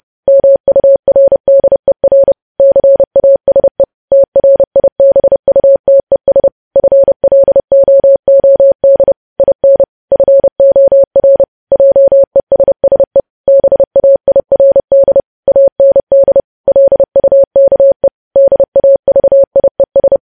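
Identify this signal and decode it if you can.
Morse code, heard as a keyed tone, about 30 wpm, 550 Hz. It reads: MURDER CASE TRIBUTES FLOOD IN FOR JESSE BAIRD AND LUKE DAVIES
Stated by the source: News headlines sent in Morse code at 30.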